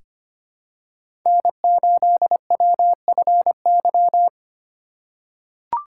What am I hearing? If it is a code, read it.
Morse code, heard as a keyed tone, about 25 wpm, 700 Hz. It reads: N8WFY
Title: USA Call Signs - 5 Characters - 3x Character Spacing - 25wpm